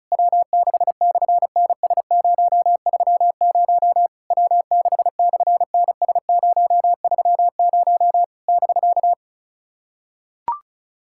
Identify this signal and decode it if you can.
Morse code, heard as a keyed tone, about 33 wpm, 700 Hz. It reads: W6/NS030 W6/NS030 <BK>
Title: SOTA QSO Practice 35wpm